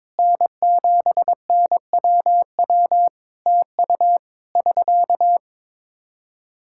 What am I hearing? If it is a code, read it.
Morse code, heard as a keyed tone, about 22 wpm, 700 Hz. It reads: N7NWW TU <SK>
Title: Practice QSO - Exchange 4 - Ending - 22wpm